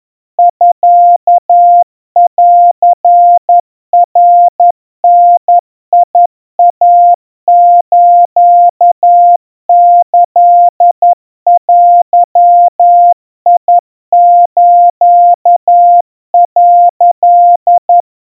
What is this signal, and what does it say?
Morse code (Japanese Wabun code), heard as a keyed tone, about 11 wpm, 700 Hz. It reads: ミンナダイスキデス」
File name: 40 cpm 700 Hz Crispy